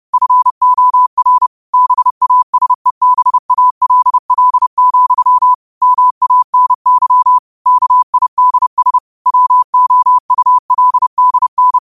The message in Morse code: FOR BASEBALL, MANY KIDS WOULDN

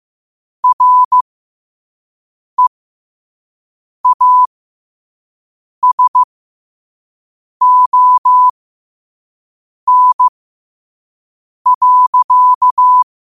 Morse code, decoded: REASON.

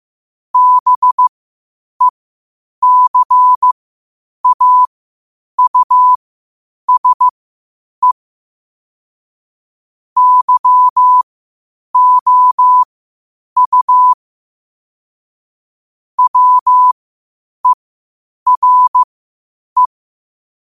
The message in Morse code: BECAUSE YOU WERE

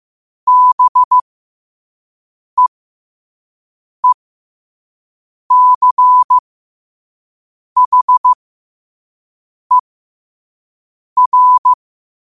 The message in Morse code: BEECHER

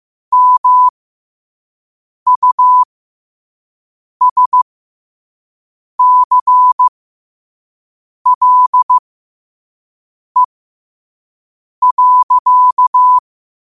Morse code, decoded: MUSCLE.

